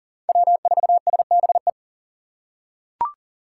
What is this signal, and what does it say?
Morse code, heard as a keyed tone, about 40 wpm, 700 Hz. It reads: W4SBE
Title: USA Call Signs - 6x Character Spacing - 40wpm